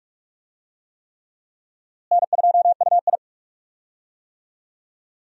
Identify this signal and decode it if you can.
Morse code, heard as a keyed tone, about 46 wpm, 700 Hz. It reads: N2UI
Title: USA Call Signs - 4 Characters - Repeated 3x - 45wpm